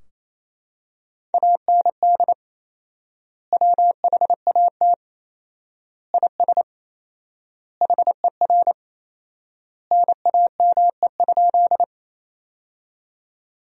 Morse code, heard as a keyed tone, about 28 wpm, 700 Hz. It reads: AND WHAT IS HER NAME?
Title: 4x Word-Spacing - Sentences from Top 200 Words - 28wpm